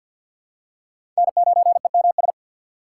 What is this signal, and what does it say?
Morse code, heard as a keyed tone, about 47 wpm, 700 Hz. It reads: N9EMS
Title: USA Call Signs - 5 Characters - 6x Character Spacing - 50wpm